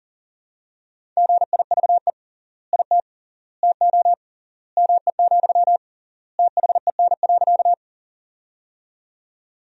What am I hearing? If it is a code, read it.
Morse code, heard as a keyed tone, about 40 wpm, 700 Hz. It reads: GIVE IT TO ME, THEN.
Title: 3x Word-Spacing - Sentences from Top 200 Words - 40wpm